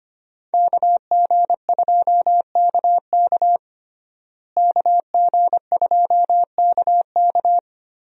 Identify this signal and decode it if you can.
Morse code, heard as a keyed tone, about 25 wpm, 700 Hz. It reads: KG2KK KG2KK
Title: USA Call Signs - 5 Characters - Repeated 3x - 25wpm